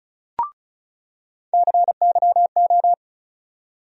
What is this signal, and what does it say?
Morse code, heard as a keyed tone, about 36 wpm, 700 Hz. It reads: CYO
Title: USA Call Sign Postfixes - 3 Letters 35wpm